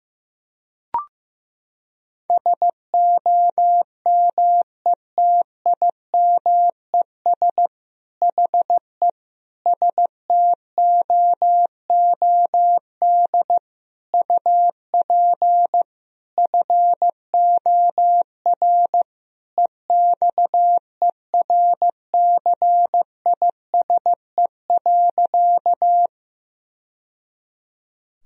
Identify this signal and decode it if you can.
Morse code, heard as a keyed tone, about 15 wpm, 700 Hz. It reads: SOMETIMES HE STOOD UP FOR EXERCISE.
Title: Sentences from Top 1000 Words - Review Part 4 - 15wpm